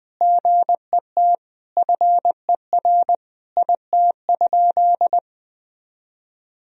Morse code, heard as a keyed tone, about 19 wpm, 700 Hz. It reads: GET FER IT?
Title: Sentences from Top 400 Words and Abbreviations 20wpm